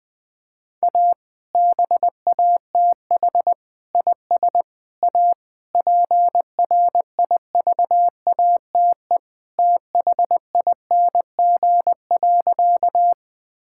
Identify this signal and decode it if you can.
Morse code, heard as a keyed tone, about 19 wpm, 700 Hz. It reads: A BATH IS A PRIVATE THING.
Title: Sentences from Top 1600 Words 20wpm